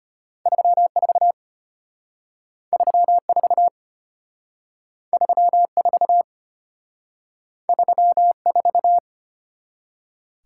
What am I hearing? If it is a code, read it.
Morse code, heard as a keyed tone, about 32 wpm, 700 Hz. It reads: 34 34 34 34